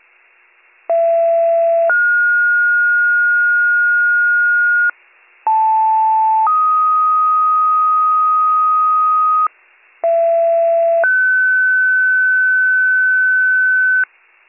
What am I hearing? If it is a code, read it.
Two-tone page, heard as a one-second tone, then a three-second tone, about 670 then 1430 Hz, about 880 then 1250 Hz, about 670 then 1500 Hz, with noise. It is heard from 0.9 to 14.0 s.